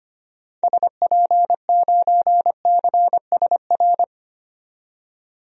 Morse code, heard as a keyed tone, about 25 wpm, 700 Hz. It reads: SP9CSR
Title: DXCC-100 Call Signs - 6 Characters - 25wpm